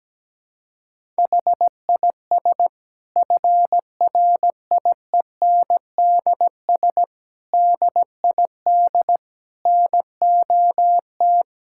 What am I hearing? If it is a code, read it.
Morse code, heard as a keyed tone, about 17 wpm, 700 Hz. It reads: HIS FRIENDS DID NOT
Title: Sentences from Top 1600 Words 17wpm